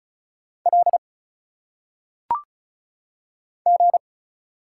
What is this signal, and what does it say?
Morse code, heard as a keyed tone, about 36 wpm, 700 Hz. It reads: L G
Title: Single Letters - Phonetic Pronunciation 35wpm